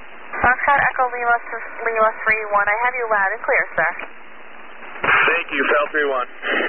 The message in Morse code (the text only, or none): none